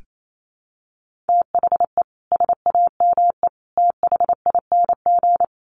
Morse code, heard as a keyed tone, about 28 wpm, 700 Hz. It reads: THE SAME THING
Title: Sets of 3 Words 28wpm